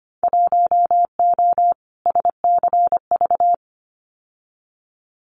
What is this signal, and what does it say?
Morse code, heard as a keyed tone, about 25 wpm, 700 Hz. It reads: 1O SCV
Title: Winter Field Day Exchanges 25wpm